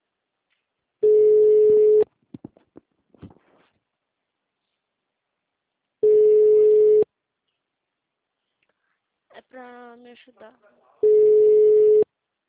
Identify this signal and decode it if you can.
Ringback tone (Europe, 425 Hz), with speech.
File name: out- -2048- - - .120